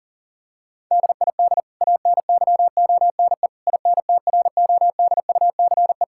Morse code, heard as a keyed tone, about 37 wpm, 700 Hz. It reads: DID ANYONE INTRODUCE
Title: Sentences from Top 1600 Words 40wpm